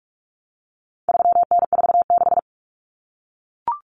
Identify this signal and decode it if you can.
Morse code, heard as a keyed tone, about 44 wpm, 700 Hz. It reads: 3N46
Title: Common Cut Numbers - 4-Digit 45wpm